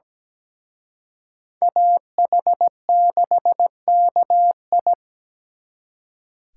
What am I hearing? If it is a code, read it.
Morse code, heard as a keyed tone, about 20 wpm, 700 Hz. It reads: AH6KI
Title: Speed Racing - USA 2x2 Call Signs 17wpm